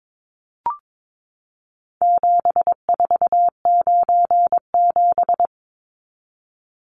Morse code, heard as a keyed tone, about 22 wpm, 700 Hz. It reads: 7497
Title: Numbers - 4-Digit 22wpm